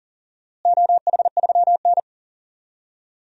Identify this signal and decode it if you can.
Morse code, heard as a keyed tone, about 40 wpm, 700 Hz. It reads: OH3N
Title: DXCC-100 Call Signs - 4 Characters - Repeated 3x - 40wpm